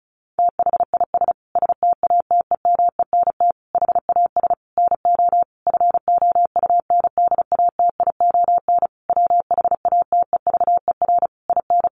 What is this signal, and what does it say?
Morse code, heard as a keyed tone, about 34 wpm, 700 Hz. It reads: THIS STATEMENT HAS NO FOUNDATION WHATEVER IN
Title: Sentences from Top 1200 Words 35wpm